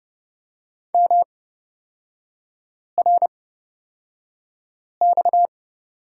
Morse code, heard as a keyed tone, about 30 wpm, spaced as for 5 wpm, 700 Hz. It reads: MRX
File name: Lesson 37e - Letter x - Three Characters 30wpm